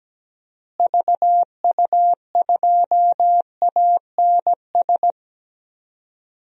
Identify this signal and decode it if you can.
Morse code, heard as a keyed tone, about 17 wpm, 700 Hz. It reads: VU2ANS
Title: DXCC-100 Call Signs - 6 Characters - Repeated 3x - 17wpm